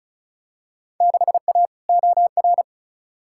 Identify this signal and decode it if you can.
Morse code, heard as a keyed tone, about 36 wpm, 700 Hz. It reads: 6A OR
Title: Field Day Exchanges - 35wpm